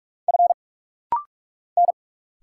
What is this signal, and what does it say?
Morse code, heard as a keyed tone, about 44 wpm, 700 Hz. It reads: F N
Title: Single Letter-Number Alt - Rapid-Fire 45wpm